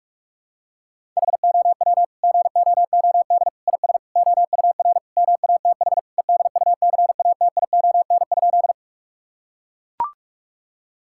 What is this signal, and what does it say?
Morse code, heard as a keyed tone, about 41 wpm, 700 Hz. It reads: HOW GOOD IS OUR MATH EDUCATION?
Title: Sentences from Top 1000 Words - Review Part 6 - 45wpm